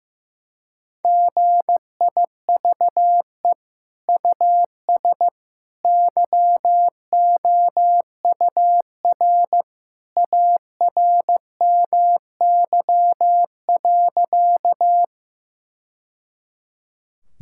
Morse code, heard as a keyed tone, about 15 wpm, 700 Hz. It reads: GIVE US YOUR ARMY.
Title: Sentences from Top 1000 Words - Review Part 5 - 15wpm